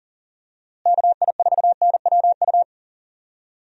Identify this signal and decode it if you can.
Morse code, heard as a keyed tone, about 40 wpm, 700 Hz. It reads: KI4NWU
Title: USA Call Signs - 6 Characters - Repeated 3x - 40wpm